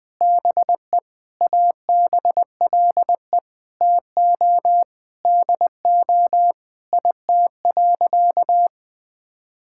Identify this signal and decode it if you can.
Morse code, heard as a keyed tone, about 21 wpm, 700 Hz. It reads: BE ABLE TO DO IT.
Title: Sentences with 4-Word Phrases - Spoken Beginning - 20wpm